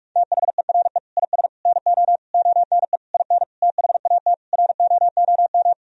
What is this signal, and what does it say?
Morse code, heard as a keyed tone, about 46 wpm, 700 Hz. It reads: THERE IS NO ONE IN THAT ROOM
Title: Sentences from Top 500 Words and Abbreviations 45wpm